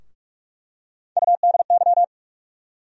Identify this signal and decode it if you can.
Morse code, heard as a keyed tone, about 46 wpm, 700 Hz. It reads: UDY
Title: USA Call Sign Postfixes - 3 Letters 45wpm